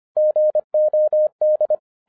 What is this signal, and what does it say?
Morse code, heard as a keyed tone, about 24 wpm, 610 Hz. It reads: GOD
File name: 2012-01-mp3 / n08-ITS MY FAULT, BUT-25wpm